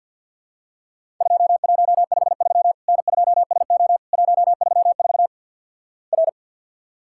Morse code, heard as a keyed tone, about 50 wpm, 700 Hz. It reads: 2153 N2SO 134 R
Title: Worked All Europe DX Contest - QTC Exchange - 50wpm